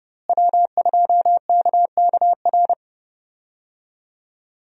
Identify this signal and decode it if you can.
Morse code, heard as a keyed tone, about 30 wpm, 700 Hz. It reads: W2KKR
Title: USA Call Signs - 5 Characters - Repeated 2x - 30wpm